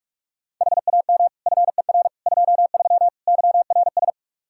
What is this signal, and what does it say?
Morse code, heard as a keyed tone, about 46 wpm, 700 Hz. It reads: HAM FER 23 YRS
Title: K1USN SST QSO Practice 45wpm